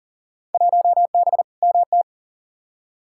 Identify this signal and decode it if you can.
Morse code, heard as a keyed tone, about 40 wpm, 700 Hz. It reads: 1B MT
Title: Field Day Exchanges - 40wpm